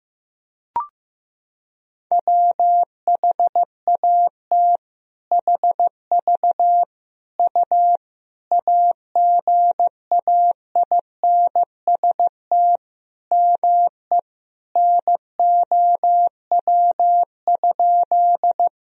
Morse code, heard as a keyed tone, about 15 wpm, 700 Hz. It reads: WHAT HV U AGAINST ME NOW?
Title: Sentences from Top 400 Words and Abbreviations 15wpm